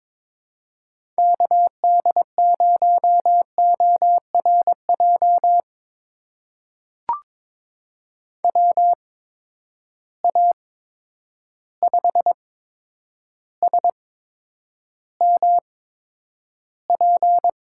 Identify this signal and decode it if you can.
Morse code, heard as a keyed tone, about 22 wpm, spaced as for 9 wpm, 700 Hz. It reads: KD0ORJ WA5SMP
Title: USA Call Signs - 6 Characters - 8x Character Spacing - 22wpm